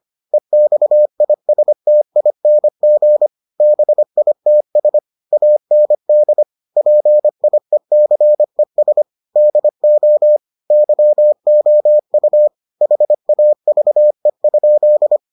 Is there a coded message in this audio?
Yes: Morse code from 0.3 to 15.2 s, heard as a keyed tone, about 25 wpm, 600 Hz. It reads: EXISTING BITS AND PIECES DO YOU HAVE?